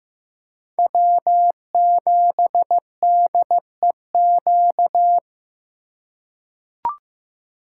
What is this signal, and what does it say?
Morse code, heard as a keyed tone, about 15 wpm, 700 Hz. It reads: W7DEQ
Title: USA Call Signs - 5 Characters - 6x Character Spacing - 15wpm